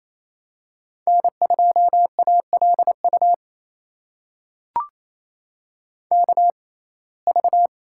Morse code, heard as a keyed tone, about 28 wpm, spaced as for 20 wpm, 700 Hz. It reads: N2ALU KV
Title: USA Call Signs - 5 Characters - 6x Character Spacing - 28wpm